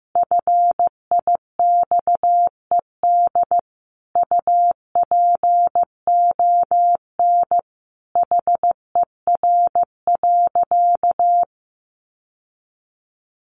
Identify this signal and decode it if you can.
Morse code, heard as a keyed tone, about 15 wpm, 700 Hz. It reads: FIXED UPON HER.
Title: Sentences from Top 1100 Words 15wpm